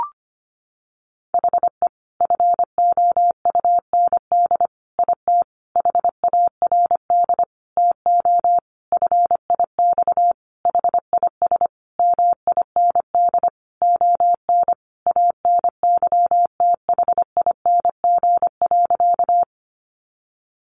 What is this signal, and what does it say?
Morse code, heard as a keyed tone, about 25 wpm, 700 Hz. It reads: HE FOUND IT HARD TO FIX HIS MIND ON ANYTHING.
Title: Sentences from Top 1400 Words 25wpm